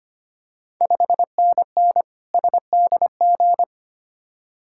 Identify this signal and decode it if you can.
Morse code, heard as a keyed tone, about 25 wpm, 700 Hz. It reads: 5NN SDG
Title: Ontario QSO Party Exchange 25wpm